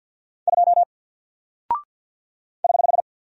Morse code, heard as a keyed tone, about 48 wpm, 700 Hz. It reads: 2 <HH>
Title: Single Character (Letter:Number:Punctuation:Prosign) Alt 50wpm